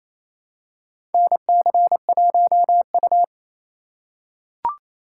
Morse code, heard as a keyed tone, about 28 wpm, 700 Hz. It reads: NC1U
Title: USA Call Signs - 4 Characters - 4x Character Spacing - 28wpm